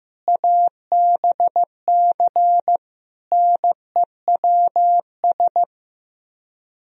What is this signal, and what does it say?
Morse code, heard as a keyed tone, about 15 wpm, 700 Hz. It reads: ABC NEWS
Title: Speed-Racing - Sets of 5 Words 15wpm